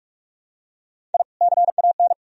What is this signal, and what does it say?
Morse code, heard as a keyed tone, about 46 wpm, 700 Hz. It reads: I CAN
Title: Sentences from Top 1000 Words - Review Part 4 - 45wpm